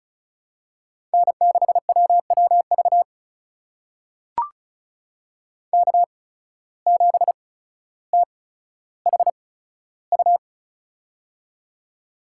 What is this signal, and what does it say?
Morse code, heard as a keyed tone, about 36 wpm, spaced as for 15 wpm, 700 Hz. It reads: N6WWV K7THU
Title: USA Call Signs - 5 Characters - 8x Character Spacing - 35wpm